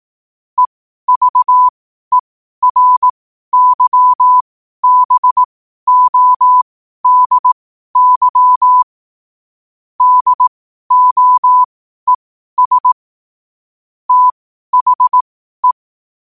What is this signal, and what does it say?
Morse code, heard as a keyed tone, about 18 wpm, spaced as for 12 wpm, 1000 Hz. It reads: EVERYBODY DOES THE
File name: qotdfiles / 12wpm